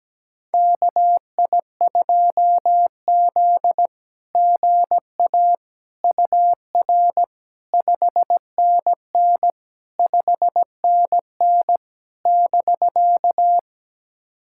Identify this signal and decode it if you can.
Morse code, heard as a keyed tone, about 17 wpm, 700 Hz. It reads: KI2Z GA UR 5NN 5NN <BK>
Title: SOTA QSO Practice 17wpm